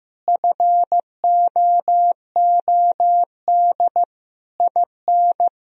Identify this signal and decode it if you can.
Morse code, heard as a keyed tone, about 16 wpm, 700 Hz. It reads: FOOD IN
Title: Sentences from Top 1000 Words - Review Part 4 - 15wpm